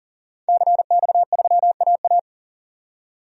Morse code, heard as a keyed tone, about 40 wpm, 700 Hz. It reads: CX3AA
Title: DXCC-100 Call Signs - 5 Characters - 40wpm